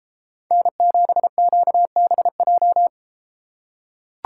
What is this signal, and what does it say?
Morse code, heard as a keyed tone, about 30 wpm, 700 Hz. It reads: N7QBJ